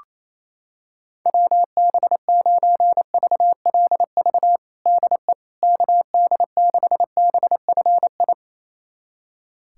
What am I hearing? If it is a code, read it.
Morse code, heard as a keyed tone, about 28 wpm, 700 Hz. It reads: WB9VLV DE KD6BFI
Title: USA Callsign DE Callsign 28wpm